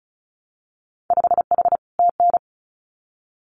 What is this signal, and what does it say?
Morse code, heard as a keyed tone, about 36 wpm, 700 Hz. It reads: 5H TN
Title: Winter Field Day Exchanges Encore 35wpm